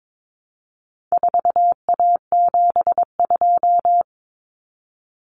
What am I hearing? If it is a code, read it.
Morse code, heard as a keyed tone, about 22 wpm, 700 Hz. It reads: 4A72
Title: Common Cut Numbers - 4-Digit 22wpm